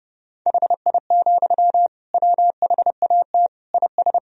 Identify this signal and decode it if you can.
Morse code, heard as a keyed tone, about 32 wpm, 700 Hz. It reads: HI, WHAT IS
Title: Sentences from Top 1000 Words - Review Part 3 - 30wpm